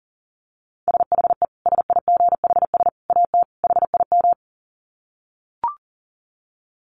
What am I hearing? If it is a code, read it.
Morse code, heard as a keyed tone, about 40 wpm, 700 Hz. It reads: SHE SIGHS AT HIM
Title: Dit Challenge 40wpm